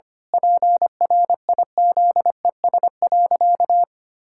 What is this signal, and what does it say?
Morse code, heard as a keyed tone, about 25 wpm, 700 Hz. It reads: PRIZES.
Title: Pangram Challenge v2 - 25wpm